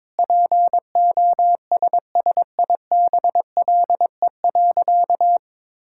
Morse code, heard as a keyed tone, about 22 wpm, 700 Hz. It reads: POSSIBLE.
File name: practice / 22wpm / Sentences with 4-Word Phrases - 22wpm